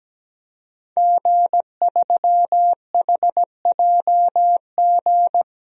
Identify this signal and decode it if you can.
Morse code, heard as a keyed tone, about 17 wpm, 700 Hz. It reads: G3HJG
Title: DXCC-100 Call Signs - 5 Characters - 17wpm